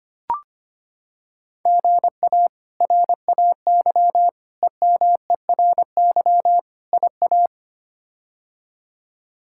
Morse code, heard as a keyed tone, about 25 wpm, 700 Hz. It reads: GA RAY EMERY IA
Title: K1USN SST QSO Practice - 25wpm